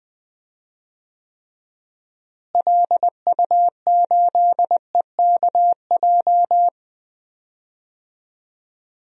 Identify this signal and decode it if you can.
Morse code, heard as a keyed tone, about 20 wpm, 700 Hz. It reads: LU8EKJ
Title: DXCC-100 Call Signs - 6 Characters - Repeated 3x - 20wpm